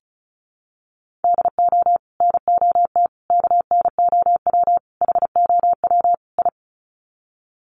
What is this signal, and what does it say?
Morse code, heard as a keyed tone, about 34 wpm, 700 Hz. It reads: DO NOT KNOW HOW I
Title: Sets of 5 Words Encore2 35wpm